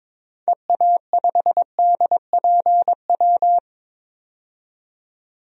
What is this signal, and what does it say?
Morse code, heard as a keyed tone, about 22 wpm, 700 Hz. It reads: EA5DPW
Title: DXCC-100 Call Signs - 6 Characters - 22wpm